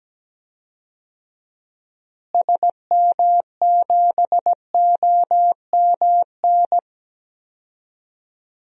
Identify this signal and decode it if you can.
Morse code, heard as a keyed tone, about 17 wpm, 700 Hz. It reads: SM7OMN
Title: DXCC-100 Call Signs - 6 Characters - Repeated 3x - 17wpm